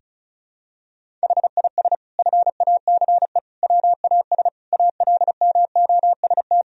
Morse code, heard as a keyed tone, about 34 wpm, 700 Hz. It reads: HIS FACE WAS ALMOST
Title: Sentences from Top 1000 Words - Review Part 4 - 35wpm